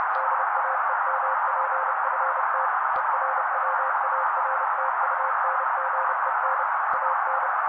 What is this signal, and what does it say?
Morse code, heard as a keyed tone, about 30 wpm, 530 Hz. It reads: DAEMPFT ERWARTUNGEN AN
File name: daily / 29wpm noise